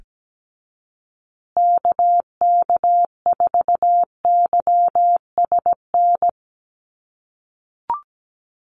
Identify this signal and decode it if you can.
Morse code, heard as a keyed tone, about 17 wpm, 700 Hz. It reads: KK4YSN
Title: USA Call Signs Repeated 2x 17wpm